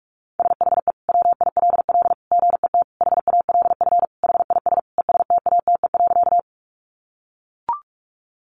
Morse code, heard as a keyed tone, about 44 wpm, 700 Hz. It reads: SHE WILL GET HALF HIS ESTATE.
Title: Sentences from Top 1200 Words 45wpm